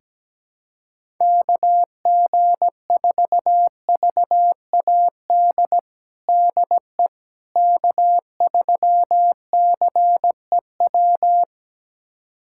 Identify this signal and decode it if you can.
Morse code, heard as a keyed tone, about 17 wpm, 700 Hz. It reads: KG4VAD DE K3CEW
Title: USA Callsign DE Callsign 17wpm